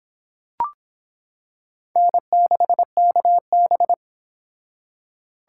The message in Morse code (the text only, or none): N6KB